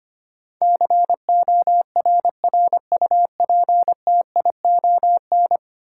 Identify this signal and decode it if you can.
Morse code, heard as a keyed tone, about 25 wpm, 700 Hz. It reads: CORRUPTION
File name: 10 Letter Words - First 3 Letters Given - 25wpm